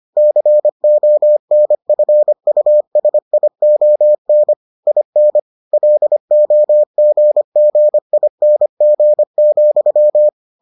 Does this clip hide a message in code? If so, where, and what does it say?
Yes: Morse code from 0.2 to 10.3 s, heard as a keyed tone, about 25 wpm, 600 Hz. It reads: CONFUSION IN LOGGING,